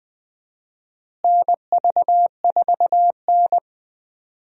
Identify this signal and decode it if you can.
Morse code, heard as a keyed tone, about 20 wpm, 700 Hz. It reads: NV4N
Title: USA Call Signs - 4 Characters - Repeated 2x - 20wpm